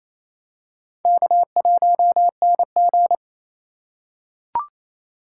Morse code, heard as a keyed tone, about 28 wpm, 700 Hz. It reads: K1NG
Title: USA Call Signs - 4 Characters - 3x Character Spacing - 28wpm